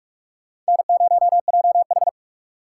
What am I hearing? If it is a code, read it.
Morse code, heard as a keyed tone, about 59 wpm, 700 Hz. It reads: N0JH